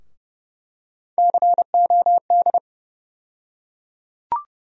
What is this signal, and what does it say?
Morse code, heard as a keyed tone, about 30 wpm, 700 Hz. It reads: COD
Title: USA Call Sign Postfixes - 3 Letters 30wpm